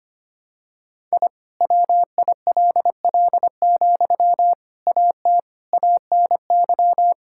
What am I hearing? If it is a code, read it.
Morse code, heard as a keyed tone, about 25 wpm, 700 Hz. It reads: I WILL, AT ANY
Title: Sentences from Top 1000 Words - Review Part 5 - 25wpm